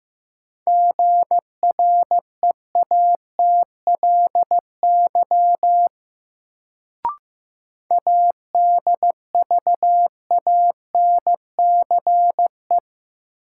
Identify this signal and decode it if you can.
Morse code, heard as a keyed tone, about 15 wpm, 700 Hz. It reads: GREATLY ADVANCE
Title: Sentences from Top 1400 Words - New Word Drill - 15wpm